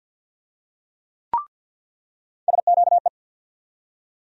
Morse code, heard as a keyed tone, about 47 wpm, 690 Hz. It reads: SQE
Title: USA Call Sign Postfixes - 3 Letters 50wpm